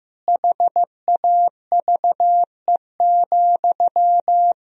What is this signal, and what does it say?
Morse code, heard as a keyed tone, about 15 wpm, 700 Hz. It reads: HAVE,
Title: Sentences from Top 1000 Words - Review Part 4 - 15wpm